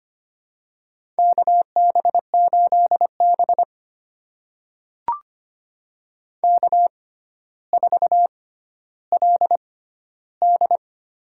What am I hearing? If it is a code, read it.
Morse code, heard as a keyed tone, about 25 wpm, spaced as for 13 wpm, 700 Hz. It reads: KB8B K4LD
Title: USA Call Signs - 4 Characters - 6x Character Spacing - 25wpm